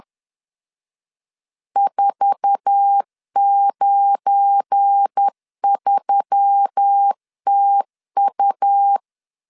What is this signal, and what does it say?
Morse code, heard as a keyed tone, about 10 wpm, 800 Hz. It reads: 493TU
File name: LEZIONI / 8 YJ45 / 8 YJ4 10wpm 2633